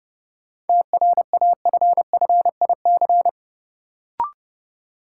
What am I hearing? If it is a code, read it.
Morse code, heard as a keyed tone, about 29 wpm, 700 Hz. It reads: TRAFFIC
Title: Sentences from Top 1300 Words - New Word Drill Alt - 30wpm